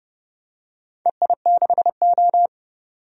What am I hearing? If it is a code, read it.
Morse code, heard as a keyed tone, about 30 wpm, 700 Hz. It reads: EI6O
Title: DXCC-100 Call Signs - 4 Characters - 30wpm